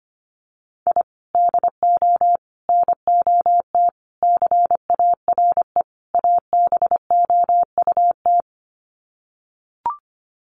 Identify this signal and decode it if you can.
Morse code, heard as a keyed tone, about 25 wpm, 700 Hz. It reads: I DO NOT CARE ABOUT
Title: Sets of 5 Words Encore2 25wpm